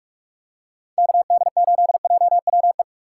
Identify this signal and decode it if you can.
Morse code, heard as a keyed tone, about 46 wpm, 700 Hz. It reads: KD8JWE
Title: USA Call Signs - 6 Characters - 4x Character Spacing - 45wpm